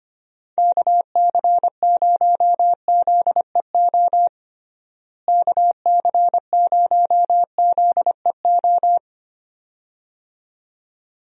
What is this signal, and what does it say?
Morse code, heard as a keyed tone, about 25 wpm, 700 Hz. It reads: KC0ZEO KC0ZEO
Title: USA Call Signs - 6 Characters - Repeated 2x - 25wpm